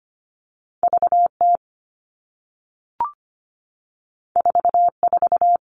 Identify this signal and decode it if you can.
Morse code, heard as a keyed tone, about 25 wpm, 700 Hz. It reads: VT 44
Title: All Cut Numbers - 2-Digit 25wpm